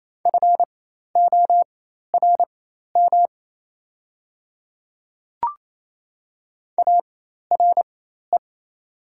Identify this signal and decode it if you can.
Morse code, heard as a keyed tone, about 27 wpm, spaced as for 13 wpm, 700 Hz. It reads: FORM ARE
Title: 4x Character Spacing - Top 200 Words - 28wpm